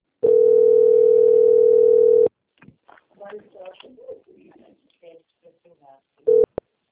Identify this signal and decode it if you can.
Ringback tone (North America), 440 + 480 Hz, with speech.
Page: Avare - SP